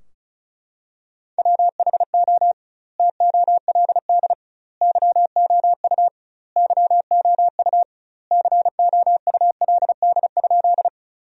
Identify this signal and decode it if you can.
Morse code, heard as a keyed tone, about 34 wpm, 700 Hz. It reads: WHO TOLD YOU YOU COULD?
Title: 2x Word-Spacing - Sentences from Top 400 Words - 35wpm